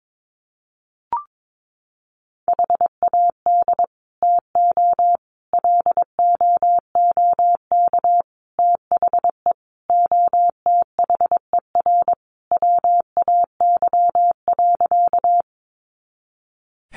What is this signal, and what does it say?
Morse code, heard as a keyed tone, about 22 wpm, 700 Hz. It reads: HAD TO LOOK THE OTHER WAY.
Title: Sentences from Top 100 Words 22wpm